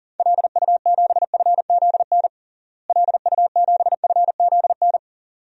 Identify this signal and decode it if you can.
Morse code, heard as a keyed tone, about 40 wpm, 700 Hz. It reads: LU7FZN LU7FZN
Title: DXCC-100 Call Signs - 6 Characters - Repeated 2x - 40wpm